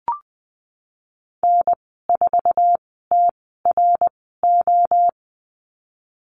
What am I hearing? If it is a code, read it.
Morse code, heard as a keyed tone, about 20 wpm, spaced as for 15 wpm, 700 Hz. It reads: N4TRO
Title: 062 Lesson 11g - Letter h - Call Signs 20wpm